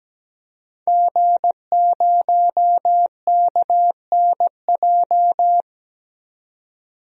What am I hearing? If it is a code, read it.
Morse code, heard as a keyed tone, about 17 wpm, 700 Hz. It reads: G0KNJ
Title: DXCC-100 Call Signs - 5 Characters - 17wpm